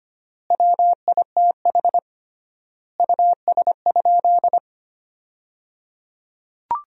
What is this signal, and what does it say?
Morse code, heard as a keyed tone, about 25 wpm, 700 Hz. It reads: WITH US?
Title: 3x Word-Spacing - Sentences from Top 200 Words - 25wpm